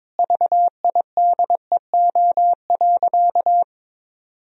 Morse code, heard as a keyed tone, about 22 wpm, 700 Hz. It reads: VIDEO.
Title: Sentences from Top 1000 Words - Review Part 4 - 22wpm